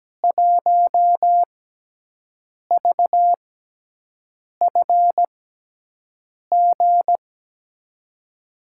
Morse code, heard as a keyed tone, about 17 wpm, spaced as for 6 wpm, 700 Hz. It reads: 1VFG
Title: USA Call Signs - 6x Character Spacing - 17wpm